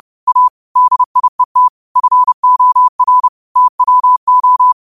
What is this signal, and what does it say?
Morse code, heard as a keyed tone, about 30 wpm, 1000 Hz. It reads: A DIET FOR TWO
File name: qotdfiles / 30wpm